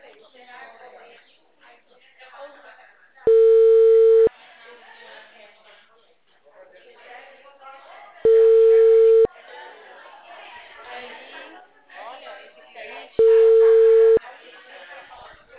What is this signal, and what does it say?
Ringback tone (China, 450 Hz), with speech.